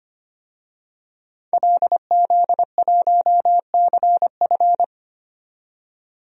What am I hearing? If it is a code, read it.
Morse code, heard as a keyed tone, about 25 wpm, 700 Hz. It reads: LZ1CF
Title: DXCC-100 Call Signs - 5 Characters - 25wpm